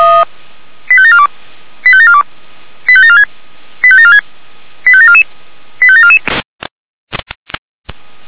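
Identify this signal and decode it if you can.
Five-tone selective call (ZVEI), heard as quick runs of five tones, tones of about 80 ms. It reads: 85742 85742 85746 85746 85744 85744
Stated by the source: ZVEI five-tone selective call 85753 received by SDR.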